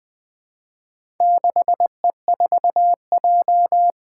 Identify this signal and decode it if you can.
Morse code, heard as a keyed tone, about 20 wpm, 700 Hz. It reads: 6E4J
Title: Four Challenging Letter-Number Combinations 20wpm